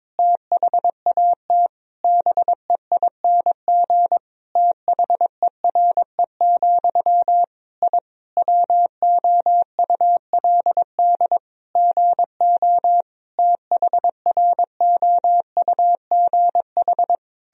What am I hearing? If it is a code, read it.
Morse code, heard as a keyed tone, about 22 wpm, 700 Hz. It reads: THAT BEING THERE, I WOULD GO THROUGH
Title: Sentences from Top 1000 Words - Review Part 3 - 22wpm